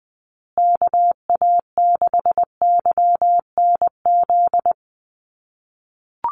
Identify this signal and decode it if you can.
Morse code, heard as a keyed tone, about 20 wpm, 700 Hz. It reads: KA6YNZ
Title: USA Call Signs Repeated 2x 20wpm